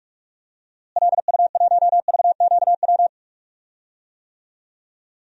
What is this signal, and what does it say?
Morse code, heard as a keyed tone, about 47 wpm, 700 Hz. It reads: LU1VQW
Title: DXCC-100 Call Signs - 6 Characters - Repeated 2x - 45wpm